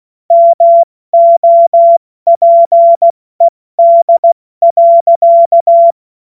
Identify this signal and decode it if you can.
Morse code, heard as a keyed tone, about 16 wpm, 680 Hz. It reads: MOPED.
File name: Witz-08-Juli-2025-70-F80